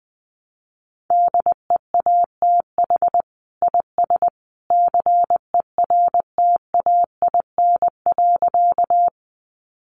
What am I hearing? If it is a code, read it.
Morse code, heard as a keyed tone, about 20 wpm, 700 Hz. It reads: DEATH IS CERTAIN.
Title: Sentences from Top 1100 Words 20wpm